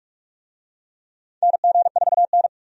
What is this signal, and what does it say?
Morse code, heard as a keyed tone, about 46 wpm, 700 Hz. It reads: NG4N
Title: USA Call Signs - 4 Characters - 45wpm